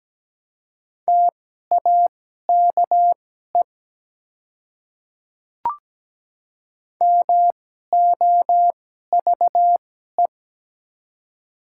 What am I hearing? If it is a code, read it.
Morse code, heard as a keyed tone, about 17 wpm, spaced as for 12 wpm, 700 Hz. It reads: TAKE MOVE
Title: 2x Character Spacing - Top 200 Words - 17wpm